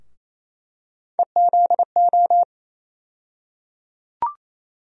Morse code, heard as a keyed tone, about 27 wpm, 700 Hz. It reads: EZO
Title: USA Call Sign Postfixes - 3 Letters 28wpm